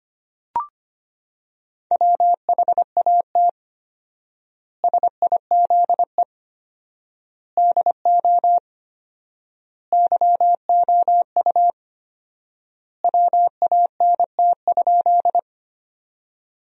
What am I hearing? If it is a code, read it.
Morse code, heard as a keyed tone, about 25 wpm, 700 Hz. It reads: WHAT SIZE DO YOU WANT?
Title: 4x Word-Spacing - Sentences from Top 400 Words - 25wpm